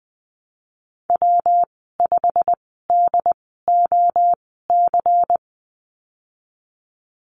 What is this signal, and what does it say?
Morse code, heard as a keyed tone, about 20 wpm, spaced as for 15 wpm, 700 Hz. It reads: W5DOC
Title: 161 Lesson 26g - Letter g - Call Signs 20wpm